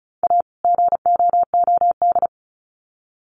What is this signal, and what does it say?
Morse code, heard as a keyed tone, about 34 wpm, 700 Hz. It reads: A GOOD
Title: Sets of 2 Words 35wpm